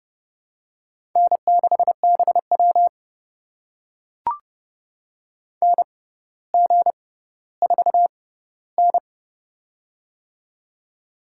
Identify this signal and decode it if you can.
Morse code, heard as a keyed tone, about 30 wpm, spaced as for 15 wpm, 700 Hz. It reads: N6BW NG4N
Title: USA Call Signs - 4 Characters - 6x Character Spacing - 30wpm